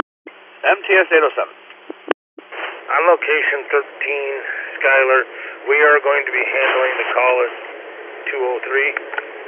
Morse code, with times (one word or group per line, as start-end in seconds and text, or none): none